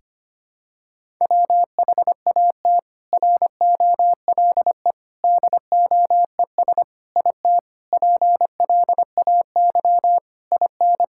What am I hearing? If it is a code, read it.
Morse code, heard as a keyed tone, about 25 wpm, 700 Hz. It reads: WHAT ROLE DOES IT PLAY IN
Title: Sentences from Top 1000 Words - Review Part 6 - 25wpm